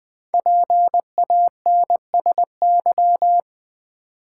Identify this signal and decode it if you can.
Morse code, heard as a keyed tone, about 20 wpm, 690 Hz. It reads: PANSY
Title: Difficult Words with Hints - 20wpm